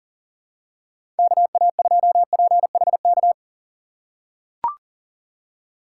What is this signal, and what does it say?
Morse code, heard as a keyed tone, about 40 wpm, 700 Hz. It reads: KA2PHK
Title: USA Call Signs - 6 Characters - 3x Character Spacing - 40wpm